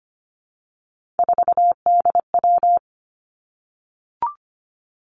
Morse code, heard as a keyed tone, about 25 wpm, 700 Hz. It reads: 4DW
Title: Three Challenging Letter-Number Combinations 25wpm